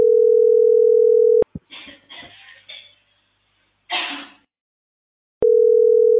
Ringback tone (North America), 440 + 480 Hz.